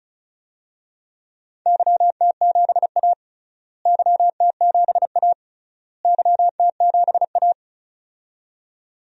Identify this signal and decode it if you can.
Morse code, heard as a keyed tone, about 36 wpm, 700 Hz. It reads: YT7A YT7A YT7A
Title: DXCC-100 Call Signs - 4 Characters - Repeated 3x - 35wpm